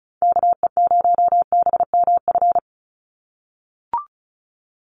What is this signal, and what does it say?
Morse code, heard as a keyed tone, about 34 wpm, 700 Hz. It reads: KE0BMF
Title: USA Call Signs 35wpm